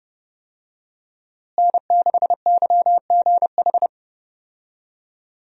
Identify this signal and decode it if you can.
Morse code, heard as a keyed tone, about 30 wpm, 700 Hz. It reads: N6YGH
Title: USA Call Signs - 5 Characters - Repeated 3x - 30wpm